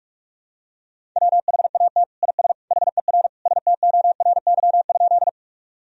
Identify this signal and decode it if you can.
Morse code, heard as a keyed tone, about 46 wpm, 700 Hz. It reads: WHAT IS HER STORY?
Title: Sentences from Top 1000 Words - Review Part 1 - 45wpm